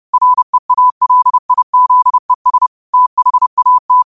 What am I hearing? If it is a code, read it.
Morse code, heard as a keyed tone, about 30 wpm, 1000 Hz. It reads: REALIZES THAT
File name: qotdfiles / 30wpm